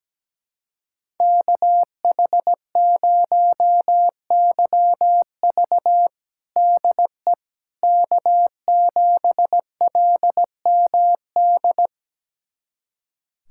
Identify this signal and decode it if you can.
Morse code, heard as a keyed tone, about 17 wpm, 700 Hz. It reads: KH0YV DE K7LMD
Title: USA Callsign DE Callsign - 5 Characters - 17wpm